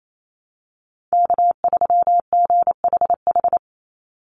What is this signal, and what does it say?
Morse code, heard as a keyed tone, about 28 wpm, 700 Hz. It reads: K3GHH
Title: Call Signs Repeated 3x - 28wpm